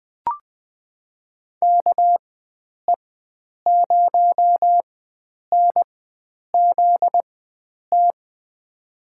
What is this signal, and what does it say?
Morse code, heard as a keyed tone, about 21 wpm, spaced as for 9 wpm, 700 Hz. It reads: KE0NZT
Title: USA Call Signs - 6 Characters - 4x Character Spacing - 20wpm